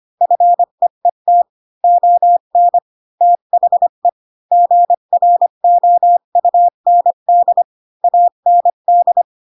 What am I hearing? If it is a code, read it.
Morse code, heard as a keyed tone, about 25 wpm, 700 Hz. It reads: FEET ON THE GROUND AND
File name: Pinocchio / Pinocchio ch5 20 25 ik1hge